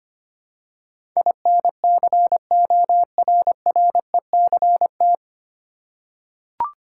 Morse code, heard as a keyed tone, about 25 wpm, 700 Hz. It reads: INCORRECT
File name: practice / 25wpm / 9 Letter Words - First 3 Letters Given - 25wpm